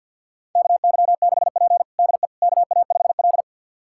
Morse code, heard as a keyed tone, about 50 wpm, 700 Hz. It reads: KY6P DE KA5L
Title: USA Callsign DE Callsign - 4 Characters - 50wpm